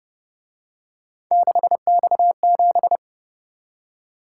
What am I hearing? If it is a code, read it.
Morse code, heard as a keyed tone, about 30 wpm, 700 Hz. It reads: 6X7